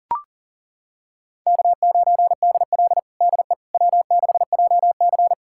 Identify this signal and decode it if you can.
Morse code, heard as a keyed tone, about 37 wpm, 700 Hz. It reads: K9DL DE W6JC
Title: USA Callsign DE Callsign - 4 Characters - 40wpm